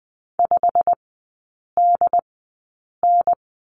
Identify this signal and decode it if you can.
Morse code, heard as a keyed tone, about 20 wpm, spaced as for 8 wpm, 700 Hz. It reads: 5DN
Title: 087 Lesson 15e - Number 5 - Three Characters 20wpm